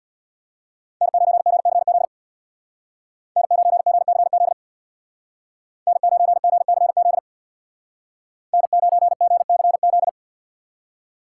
Morse code, heard as a keyed tone, about 64 wpm, 700 Hz. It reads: N9GCZ N9GCZ N9GCZ N9GCZ